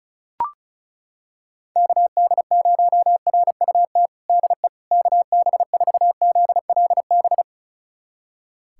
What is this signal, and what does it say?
Morse code, heard as a keyed tone, about 36 wpm, 690 Hz. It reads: KD0RUT DE KB4ZLB
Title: USA Callsign DE Callsign - 6 Characters - 35wpm